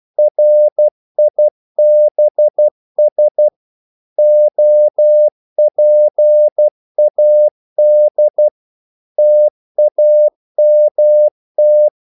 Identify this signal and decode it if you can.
Morse code, heard as a keyed tone, about 12 wpm, 600 Hz. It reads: RIBS OPAD TAMT
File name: Korte ord DK 12wpm